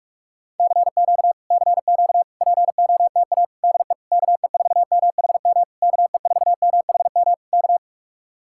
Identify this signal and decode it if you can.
Morse code, heard as a keyed tone, about 49 wpm, 700 Hz. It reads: CQ CQ POTA DE KE4MHM KE4MHM K